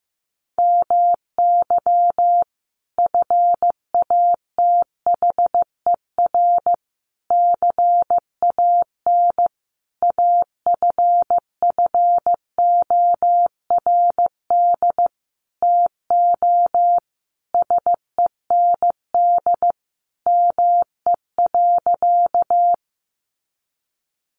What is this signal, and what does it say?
Morse code, heard as a keyed tone, about 15 wpm, 700 Hz. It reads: MY FATHER CAN AFFORD TO SEND ME.
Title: Sentences from Top 1500 Words 15wpm